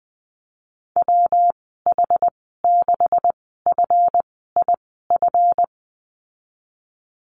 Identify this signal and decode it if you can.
Morse code, heard as a keyed tone, about 20 wpm, spaced as for 15 wpm, 700 Hz. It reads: WH6FIF
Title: 186 Lesson 30g - Letter b - Call Signs 20wpm